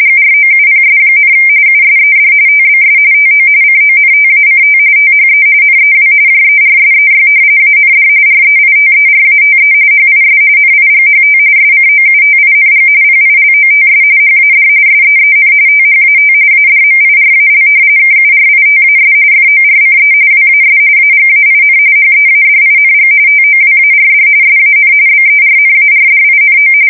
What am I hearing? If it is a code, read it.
RTTY, about 45.45 baud. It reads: GROUPS, USED TBS TO CONTACT THE NEW JERSEY AND REITERATE HIS NIGHT SCOUTING PILOTS HAD SEEN THAT ADMIRAL KURITAS FORCE WAS AGAIN HEADING TOWARD SAN BERNARDINO
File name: fsk170 / battleshipnj